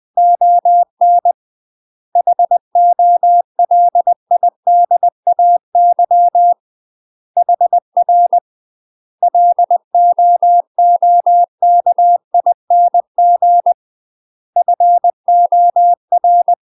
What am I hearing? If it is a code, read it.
Morse code, heard as a keyed tone, about 20 wpm, 700 Hz. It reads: ON HOLIDAY HR LOOKING FOR
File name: UK MASTER QSOs TRAINING / 20WPM / UK MASTER QSOs 0012